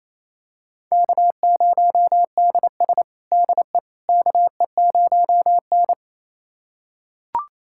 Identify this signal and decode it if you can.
Morse code, heard as a keyed tone, about 28 wpm, 700 Hz. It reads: K0DS DE KE0N
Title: USA Callsign DE Callsign - 4 Characters - 28wpm